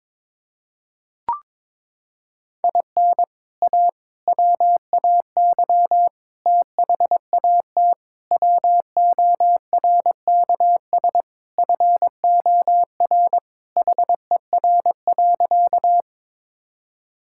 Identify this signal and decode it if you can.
Morse code, heard as a keyed tone, about 21 wpm, 700 Hz. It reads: IN A WAY THAT WORKS FOR HER.
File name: practice / 22wpm / Sentences with 4-Word Phrases - 22wpm